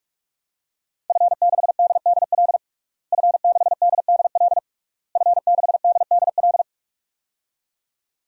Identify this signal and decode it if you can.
Morse code, heard as a keyed tone, about 46 wpm, 700 Hz. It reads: F6DDL F6DDL F6DDL
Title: DXCC-100 Call Signs - 5 Characters - Repeated 3x - 45wpm